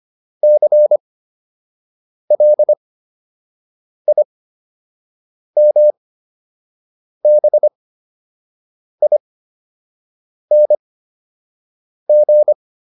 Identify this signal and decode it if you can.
Morse code, heard as a keyed tone, about 25 wpm, spaced as for 6 wpm, 600 Hz. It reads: CLIMBING